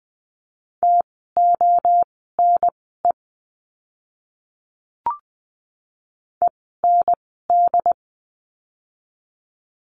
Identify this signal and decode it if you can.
Morse code, heard as a keyed tone, about 20 wpm, spaced as for 15 wpm, 700 Hz. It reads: TONE END
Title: 068 Lesson 12f - Letter d - Words 20wpm